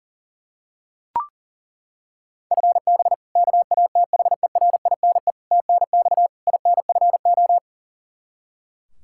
Morse code, heard as a keyed tone, about 40 wpm, 700 Hz. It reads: FB KATHERINE TNX INFO
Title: Practice QSO - Exchange 3 - More Info - 40wpm